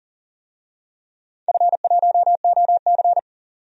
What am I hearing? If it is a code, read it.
Morse code, heard as a keyed tone, about 44 wpm, 690 Hz. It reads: F1OC
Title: DXCC-100 Call Signs - 4 Characters - 40wpm